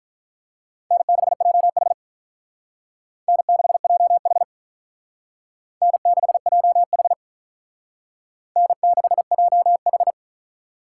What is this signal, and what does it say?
Morse code, heard as a keyed tone, about 44 wpm, 700 Hz. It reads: N6JH N6JH N6JH N6JH